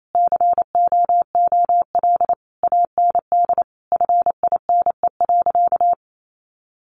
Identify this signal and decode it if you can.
Morse code, heard as a keyed tone, about 28 wpm, 700 Hz. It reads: COOL AND FINE.
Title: Sentences from Top 1200 Words 28wpm